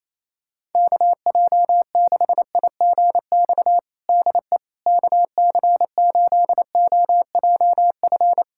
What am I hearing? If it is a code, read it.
Morse code, heard as a keyed tone, about 28 wpm, 700 Hz. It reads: KJ6IGX DE KC8OJF
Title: USA Callsign DE Callsign - 6 Characters - 28wpm